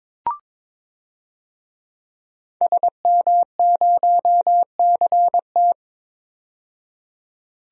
Morse code, heard as a keyed tone, about 22 wpm, 700 Hz. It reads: SM0CT
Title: DXCC-100 Call Signs - 5 Characters - 22wpm